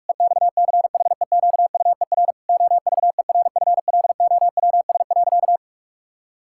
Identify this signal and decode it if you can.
Morse code, heard as a keyed tone, about 46 wpm, 700 Hz. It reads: EXCHEQUER OVERFLOWS.
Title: Pangram Challenge v2 - 45wpm